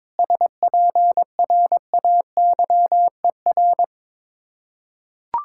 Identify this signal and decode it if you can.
Morse code, heard as a keyed tone, about 22 wpm, 690 Hz. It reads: SPRAYER
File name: Difficult Words with Hints - 22wpm